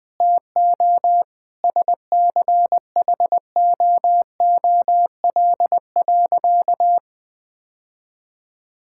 Morse code, heard as a keyed tone, about 21 wpm, 700 Hz. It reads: TO SCHOOL.
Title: Sentences with 4-Word Phrases - 20wpm